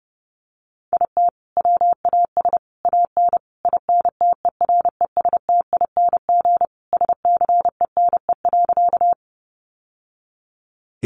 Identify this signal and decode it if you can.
Morse code, heard as a keyed tone, about 30 wpm, 700 Hz. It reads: IT WAS AN INTERESTING SCENE.
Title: Sentences from Top 1100 Words 30wpm